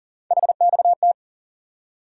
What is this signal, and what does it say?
Morse code, heard as a keyed tone, about 40 wpm, 700 Hz. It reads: HXT
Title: USA Call Sign Postfixes - 3 Letters 40wpm